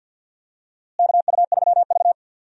Morse code, heard as a keyed tone, about 46 wpm, 700 Hz. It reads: KU3V
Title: USA Call Signs - 4 Characters - Repeated 2x - 50wpm